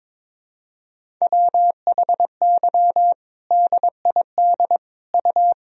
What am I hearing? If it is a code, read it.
Morse code, heard as a keyed tone, about 22 wpm, 700 Hz. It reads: WHY DID U
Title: Sentences from Top 500 Words and Abbreviations 22wpm